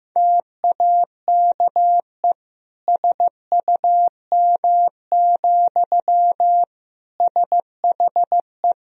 Morse code, heard as a keyed tone, about 15 wpm, 700 Hz. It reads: TAKE SUM, SHE
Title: Sentences from Top 200 Words and Abbreviations 15wpm